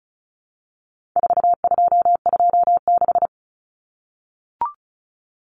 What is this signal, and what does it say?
Morse code, heard as a keyed tone, about 34 wpm, 700 Hz. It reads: 4226
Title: Common Cut Numbers - 4-Digit 35wpm